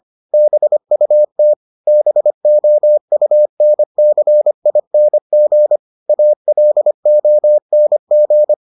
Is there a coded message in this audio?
Yes: Morse code from 0.3 to 8.5 s, heard as a keyed tone, about 25 wpm, 600 Hz. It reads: BUT BOUNCING ALONG